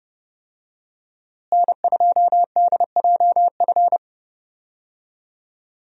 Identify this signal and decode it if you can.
Morse code, heard as a keyed tone, about 31 wpm, 700 Hz. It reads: N2DJF
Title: USA Call Signs - 5 Characters - Repeated 3x - 30wpm